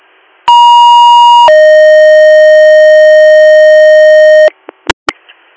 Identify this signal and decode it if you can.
Two-tone page, heard as a one-second tone, then a three-second tone, about 950 then 620 Hz, with noise. It is heard from 0.5 to 4.5 s.